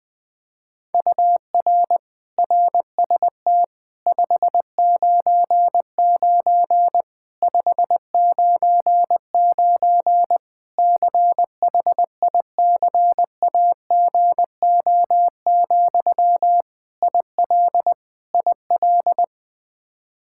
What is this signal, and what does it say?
Morse code, heard as a keyed tone, about 20 wpm, 700 Hz. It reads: UR RST 599 599 CHICAGO, IL IL
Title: Practice QSO - Exchange 1 - Beginning - 20wpm